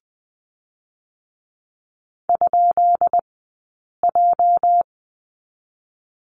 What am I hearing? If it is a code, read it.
Morse code, heard as a keyed tone, about 20 wpm, 700 Hz. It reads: ? J
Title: 204 Lesson 33d - Letter j - Two Characters 20wpm